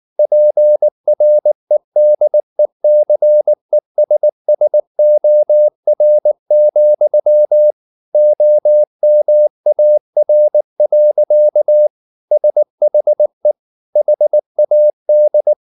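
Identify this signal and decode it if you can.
Morse code, heard as a keyed tone, about 19 wpm, 600 Hz. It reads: PREDECESSOR, OMAR. SHE HAD
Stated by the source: Morse code at 19 wpm from CW med Gnister, reading Ken Follett's novel Never in English.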